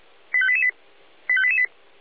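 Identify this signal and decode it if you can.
Five-tone selective call (ZVEI), heard as quick runs of five tones, tones of about 70 ms. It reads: 85888 85888
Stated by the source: ZVEI five-tone selective call received on an SDR.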